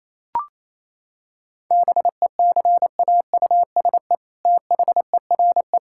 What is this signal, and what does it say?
Morse code, heard as a keyed tone, about 28 wpm, 700 Hz. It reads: BECAUSE THERE
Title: Sentences from Top 1600 Words 28wpm